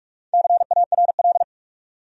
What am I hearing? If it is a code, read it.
Morse code, heard as a keyed tone, about 42 wpm, 700 Hz. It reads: CARL
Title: USA Common Names - Max 5 Letters - 45wpm